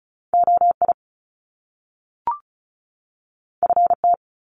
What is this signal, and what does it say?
Morse code, heard as a keyed tone, about 36 wpm, 700 Hz. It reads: OI FT
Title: Top Two Letter Combinations in Words 35wpm